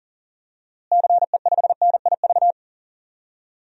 Morse code, heard as a keyed tone, about 40 wpm, 700 Hz. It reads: CE5NIV
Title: DXCC-100 Call Signs - 6 Characters - Repeated 3x - 40wpm